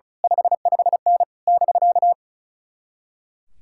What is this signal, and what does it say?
Morse code, heard as a keyed tone, about 33 wpm, 700 Hz. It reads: 55N <BK>
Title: Speed-Racing Common QSO Phrases Encore - 35wpm